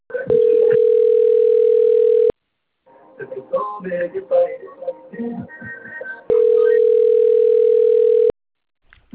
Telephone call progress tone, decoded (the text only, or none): ringback tone